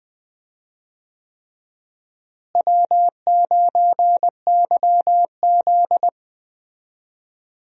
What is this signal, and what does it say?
Morse code, heard as a keyed tone, about 20 wpm, 700 Hz. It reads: W9YZ
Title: USA Call Signs - 4 Characters - Repeated 3x - 20wpm